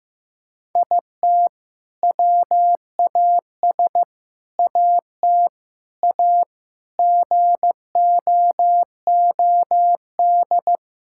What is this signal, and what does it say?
Morse code, heard as a keyed tone, about 15 wpm, 700 Hz. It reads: IT WAS AT A GOOD
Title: Sentences from Top 1000 Words - Review Part 5 - 15wpm